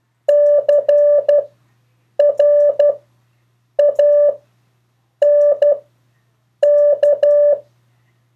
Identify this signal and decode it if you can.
Morse code, heard as a keyed tone, about 12 wpm, spaced as for 7 wpm, 580 Hz. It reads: CRANK
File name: 2of4-words-1